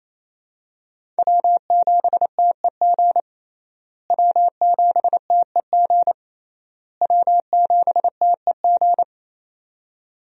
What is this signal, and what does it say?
Morse code, heard as a keyed tone, about 28 wpm, 700 Hz. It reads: W7TEG W7TEG W7TEG
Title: USA Call Signs - 5 Characters - Repeated 3x - 28wpm